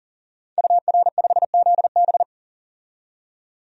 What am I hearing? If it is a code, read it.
Morse code, heard as a keyed tone, about 40 wpm, 700 Hz. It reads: UR5ZB
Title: DXCC-100 Call Signs - 5 Characters - 40wpm